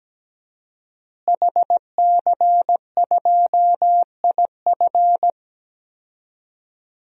Morse code, heard as a keyed tone, about 17 wpm, 700 Hz. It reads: HC2IF
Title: DXCC-100 Call Signs - 5 Characters - Repeated 2x - 17wpm